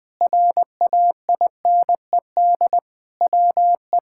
Morse code, heard as a keyed tone, about 20 wpm, 700 Hz. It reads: RAINED WE
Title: Sentences with 6-Word Phrases - 20wpm